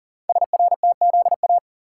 Morse code, heard as a keyed tone, about 42 wpm, 700 Hz. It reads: SRTZA
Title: Code Groups - 5 Letters - 40wpm